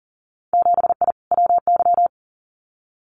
Morse code, heard as a keyed tone, about 38 wpm, 700 Hz. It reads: 7I WY
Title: Winter Field Day Exchanges Encore 40wpm